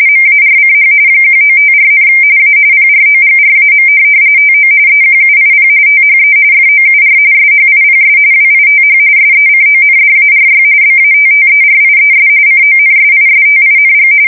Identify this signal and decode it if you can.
RTTY, about 45.45 baud. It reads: MARCH 1954, A PROGRAM TO EQUIP THE FOUR SHIPS WITH SECONDARY BATTERIES CONSISTING
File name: fsk170 / USS-Massachusetts